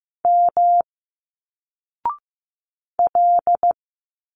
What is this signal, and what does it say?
Morse code, heard as a keyed tone, about 15 wpm, 700 Hz. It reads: M L
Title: Single Letters 15wpm